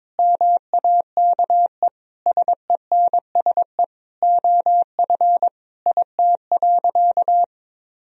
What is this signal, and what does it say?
Morse code, heard as a keyed tone, about 22 wpm, 700 Hz. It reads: MAKE SENSE OF IT.
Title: Sentences from Top 1000 Words - Review Part 4 - 22wpm